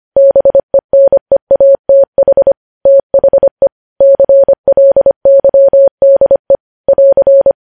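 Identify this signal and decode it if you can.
Morse code, heard as a keyed tone, about 25 wpm, 550 Hz. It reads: BENEATH THE CLYDE <AR>
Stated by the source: Morse code news headlines sent at 25 wpm.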